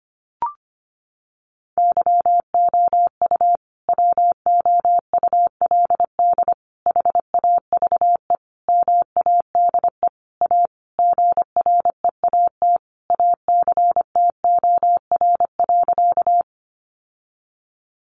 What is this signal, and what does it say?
Morse code, heard as a keyed tone, about 25 wpm, 700 Hz. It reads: YOU WOULD HAVE MADE A GREAT ACTOR.
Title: Sentences from Top 1500 Words 25wpm